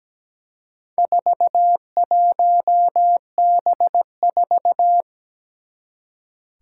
Morse code, heard as a keyed tone, about 17 wpm, 700 Hz. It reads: 41B4
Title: Four Challenging Letter-Number Combinations 17wpm